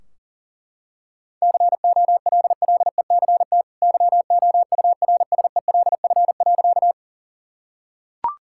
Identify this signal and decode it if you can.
Morse code, heard as a keyed tone, about 40 wpm, 700 Hz. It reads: COLLECT YOURSELF.
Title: Sentences from Top 1600 Words 40wpm